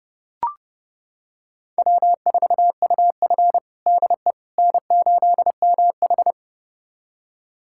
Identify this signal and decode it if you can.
Morse code, heard as a keyed tone, about 32 wpm, 700 Hz. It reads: W4UF DE N8MH
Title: USA Callsign DE Callsign - 4 Characters - 30wpm